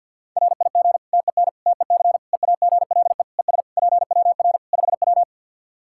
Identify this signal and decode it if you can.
Morse code, heard as a keyed tone, about 53 wpm, 700 Hz. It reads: RIG TEN TEC EAGLE ES PWR 5W
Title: Practice QSO - Exchange 2 - Info - 50wpm